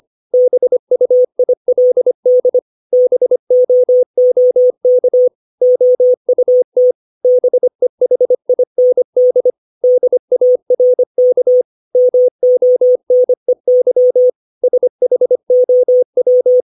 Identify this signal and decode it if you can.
Morse code, heard as a keyed tone, about 25 wpm, 500 Hz. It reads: BUILD BOOK OUT BEHIND DARK MONEY SHOW
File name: 632 words / 632 Words 25WPM